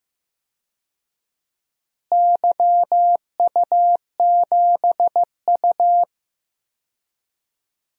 Morse code, heard as a keyed tone, about 15 wpm, 700 Hz. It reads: YU7U
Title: DXCC-100 Call Signs - 4 Characters - Repeated 3x - 15wpm